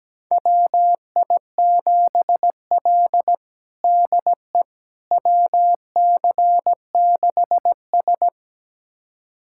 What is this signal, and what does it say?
Morse code, heard as a keyed tone, about 17 wpm, 700 Hz. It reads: WI7L DE WC6S
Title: USA Callsign DE Callsign - 4 Characters - 17wpm